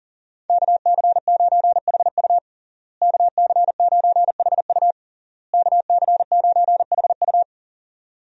Morse code, heard as a keyed tone, about 40 wpm, 700 Hz. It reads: KC9HU KC9HU KC9HU
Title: USA Call Signs - 5 Characters - Repeated 3x - 40wpm